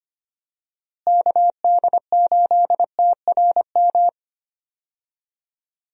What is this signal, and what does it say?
Morse code, heard as a keyed tone, about 25 wpm, 700 Hz. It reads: KD8TRM
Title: USA Call Signs - 6 Characters - Repeated 2x - 25wpm